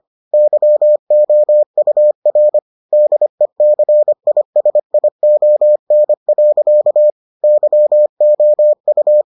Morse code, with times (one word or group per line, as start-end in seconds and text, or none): 0.3-2.6 YOUR
2.9-7.1 DECISION.
7.4-9.2 YOU